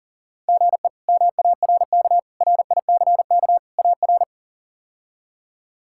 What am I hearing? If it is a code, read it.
Morse code, heard as a keyed tone, about 40 wpm, 700 Hz. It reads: GE MARK RICK AR
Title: K1USN SST QSO Practice - 40wpm